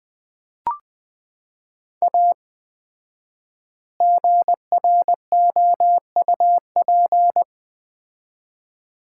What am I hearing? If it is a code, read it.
Morse code, heard as a keyed tone, about 21 wpm, 700 Hz. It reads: A GROUP
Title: 4x Word-Spacing - Sets of 3 Words - 20wpm